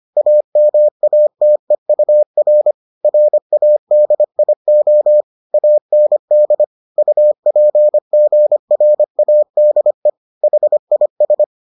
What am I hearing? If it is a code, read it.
Morse code, heard as a keyed tone, about 25 wpm, 600 Hz. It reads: AMATEUR RADIO AND UPGRADE HIS